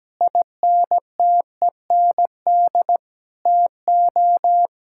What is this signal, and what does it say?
Morse code, heard as a keyed tone, about 17 wpm, 700 Hz. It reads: INTEND TO
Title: Sentences from Top 1600 Words 17wpm